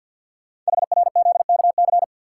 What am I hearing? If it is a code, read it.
Morse code, heard as a keyed tone, about 52 wpm, 700 Hz. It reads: HRZKC
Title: Code Groups - 5 Letters - 50wpm